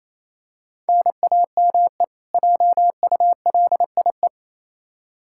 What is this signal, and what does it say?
Morse code, heard as a keyed tone, about 28 wpm, 700 Hz. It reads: NAME JULIE
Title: K1USN SST QSO Practice 28wpm